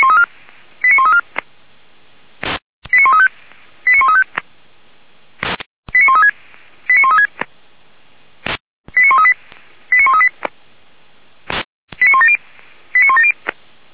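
Five-tone selective call (ZVEI), heard as quick runs of five tones, tones of about 70 ms. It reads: 89134 89135 89135 89136 89136 89138 89138 89170 89170
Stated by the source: ZVEI five-tone selective call received on SDR, logged by feuerwehr-orlamuende (fire brigade Orlamünde).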